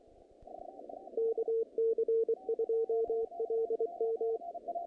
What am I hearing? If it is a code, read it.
Morse code, heard as a keyed tone, about 24 wpm, 450 Hz, with radio noise. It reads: KC2LM
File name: contests / naqp-cw-k5tr-op-k5pi- -2100